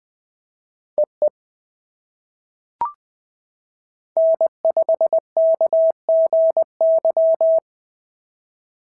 Morse code, heard as a keyed tone, about 20 wpm, 690 Hz. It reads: EE N5KGY
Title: POTA QSO Practice - 20wpm